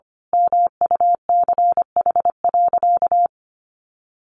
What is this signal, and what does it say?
Morse code, heard as a keyed tone, about 25 wpm, 700 Hz. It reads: MUCH.
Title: Sentences from Top 1200 Words 25wpm